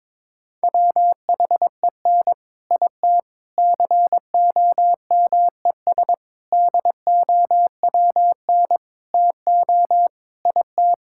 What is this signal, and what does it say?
Morse code, heard as a keyed tone, about 22 wpm, 700 Hz. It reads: WHEN IT COMES DOWN TO IT
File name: practice / 22wpm / Sets of 6 Words - 22wpm